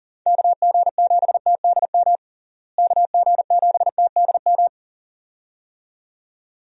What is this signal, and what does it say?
Morse code, heard as a keyed tone, about 40 wpm, 700 Hz. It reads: KG7TDM KG7TDM
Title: USA Call Signs - 6 Characters - Repeated 2x - 40wpm